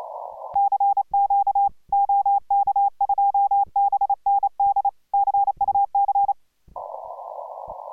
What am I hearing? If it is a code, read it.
Morse code, heard as a keyed tone, about 29 wpm, 800 Hz. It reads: CQ OK2BND CUC